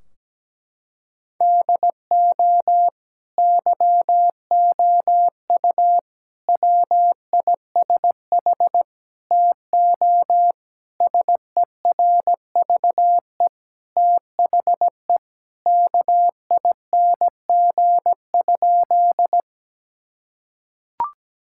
Morse code, heard as a keyed tone, about 17 wpm, 700 Hz. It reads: DO YOU WISH TO SERVE THE KING?
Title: Sentences from Top 1600 Words 17wpm